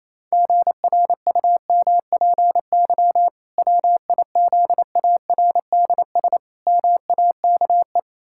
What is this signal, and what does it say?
Morse code, heard as a keyed tone, about 28 wpm, 700 Hz. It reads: GRUMPY WIZARDS MAKE
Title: Pangram Challenge v2 - 28wpm